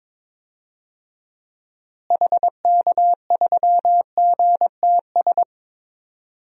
Morse code, heard as a keyed tone, about 22 wpm, 700 Hz. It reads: HK3GTS
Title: DXCC-100 Call Signs - 6 Characters - Repeated 3x - 22wpm